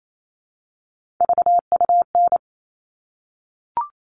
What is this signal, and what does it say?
Morse code, heard as a keyed tone, about 28 wpm, 700 Hz. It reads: VUN
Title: All Cut Numbers - 3-Digit 28wpm